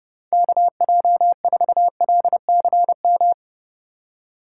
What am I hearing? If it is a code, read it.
Morse code, heard as a keyed tone, about 30 wpm, 700 Hz. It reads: KJ4LCM
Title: USA Call Signs - 6 Characters - 3x Character Spacing - 30wpm